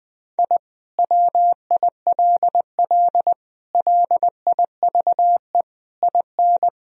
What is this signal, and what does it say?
Morse code, heard as a keyed tone, about 20 wpm, 700 Hz. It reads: I WILL LIVE IN
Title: Sentences from Top 1000 Words - Review Part 2 - 20wpm